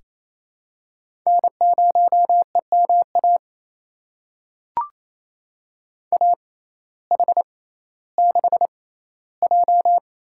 Morse code, heard as a keyed tone, about 28 wpm, spaced as for 15 wpm, 700 Hz. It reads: N0EMA AH6J
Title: USA Call Signs - 5 Characters - 6x Character Spacing - 28wpm